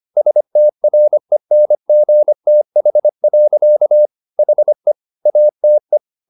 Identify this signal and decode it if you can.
Morse code, heard as a keyed tone, about 25 wpm, 600 Hz. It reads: STRENGTH. HE ATE
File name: the old man and the sea part 1 25wpm